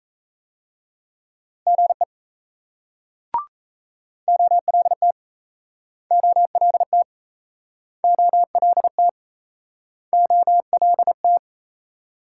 Morse code, heard as a keyed tone, about 37 wpm, 700 Hz. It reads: GE OLT OLT OLT OLT